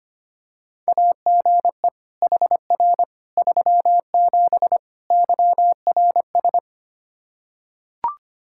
Morse code, heard as a keyed tone, about 25 wpm, 700 Hz. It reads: AGE HR 37 YRS
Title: Practice QSO - Exchange 3 - More Info - 25wpm